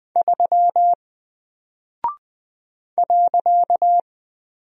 Morse code, heard as a keyed tone, about 20 wpm, 700 Hz. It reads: 3 .
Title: Single Character (Letter:Number:Punctuation:Prosign) Alt 20wpm